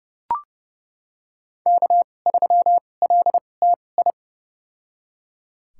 Morse code, heard as a keyed tone, about 31 wpm, spaced as for 22 wpm, 700 Hz. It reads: K3LTI
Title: USA Call Signs - 2x Character Spacing - 30wpm